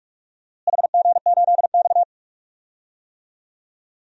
Morse code, heard as a keyed tone, about 42 wpm, 700 Hz. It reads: HG8X
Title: DXCC-100 Call Signs - 4 Characters - Repeated 2x - 45wpm